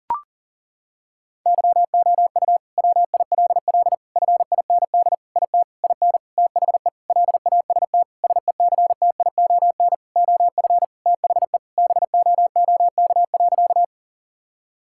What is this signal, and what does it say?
Morse code, heard as a keyed tone, about 40 wpm, 700 Hz. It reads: YOU WILL FIND IT IN THE LAST SECTION OF THE BOOK.
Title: Sentences from Top 1000 Words - Review Part 5 - 40wpm